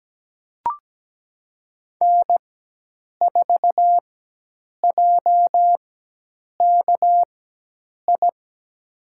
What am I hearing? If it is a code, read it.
Morse code, heard as a keyed tone, about 17 wpm, spaced as for 8 wpm, 700 Hz. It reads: N4JKI
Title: USA Call Signs - 5 Characters - 4x Character Spacing - 17wpm